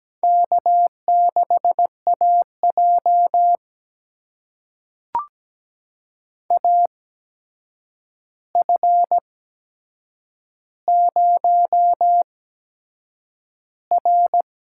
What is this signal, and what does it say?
Morse code, heard as a keyed tone, about 17 wpm, spaced as for 7 wpm, 700 Hz. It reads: K6AJ AF0R
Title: USA Call Signs - 4 Characters - 8x Character Spacing - 17wpm